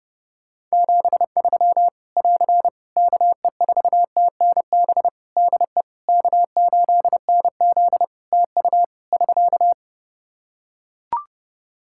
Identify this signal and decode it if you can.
Morse code, heard as a keyed tone, about 30 wpm, 700 Hz. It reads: 73 <AR> KE4TNB DE K8NZ TU <SK>
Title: Practice QSO - Exchange 4 - Ending - 30wpm